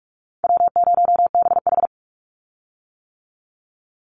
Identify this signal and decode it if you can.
Morse code, heard as a keyed tone, about 44 wpm, 700 Hz. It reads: W0BH
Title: Call Signs - 45wpm